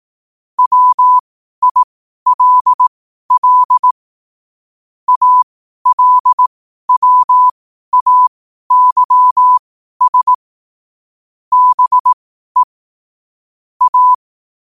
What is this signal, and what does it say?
Morse code, heard as a keyed tone, about 18 wpm, spaced as for 12 wpm, 1000 Hz. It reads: WILL ALWAYS BE A
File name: qotdfiles / 12wpm